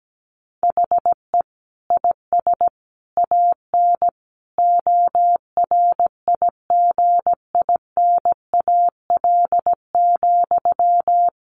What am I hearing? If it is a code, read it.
Morse code, heard as a keyed tone, about 17 wpm, 700 Hz. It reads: HE IS AN ORIGINAL,
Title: Sentences from Top 900 Words 17wpm